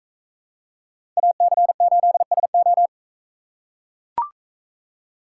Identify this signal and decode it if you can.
Morse code, heard as a keyed tone, about 41 wpm, 700 Hz. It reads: AC8SO